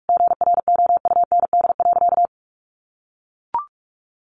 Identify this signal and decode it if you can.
Morse code, heard as a keyed tone, about 44 wpm, 700 Hz. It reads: GROUND.
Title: Sentences from Top 1500 Words 45wpm